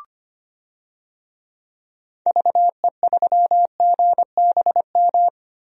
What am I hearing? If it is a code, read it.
Morse code, heard as a keyed tone, about 25 wpm, 700 Hz. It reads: VE3GBM
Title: DXCC-100 Call Signs - 6 Characters - 25wpm